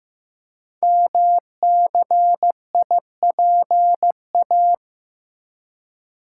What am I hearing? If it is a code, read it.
Morse code, heard as a keyed tone, about 15 wpm, 700 Hz. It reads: MCIPA
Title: Code Groups - 5 Letters - 15wpm